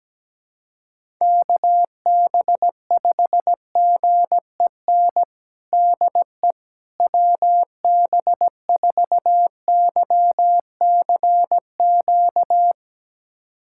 Morse code, heard as a keyed tone, about 17 wpm, 700 Hz. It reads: KB5GEN DE WB4YCQ
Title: USA Callsign DE Callsign - 6 Characters - 17wpm